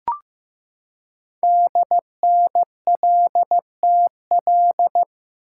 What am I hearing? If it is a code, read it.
Morse code, heard as a keyed tone, about 15 wpm, 700 Hz. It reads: DNLTL
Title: Code Groups - 5 Letters - 15wpm